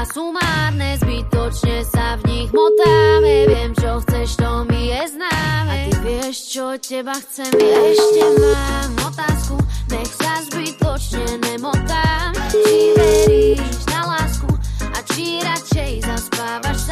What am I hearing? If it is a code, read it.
Ringback tone (Europe, 425 Hz), with noise.
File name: lUO4MhJ1zeC1Ylt6